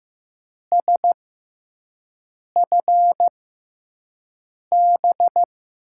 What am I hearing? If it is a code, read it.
Morse code, heard as a keyed tone, about 15 wpm, spaced as for 5 wpm, 700 Hz. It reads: SFB
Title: USA Call Signs - 6 Characters - 6x Character Spacing - 15wpm